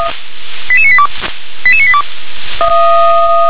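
Five-tone selective call (ZVEI), heard as quick runs of five tones, tones of about 70 ms. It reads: 88982 88982
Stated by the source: ZVEI five-tone selective call received on an SDR.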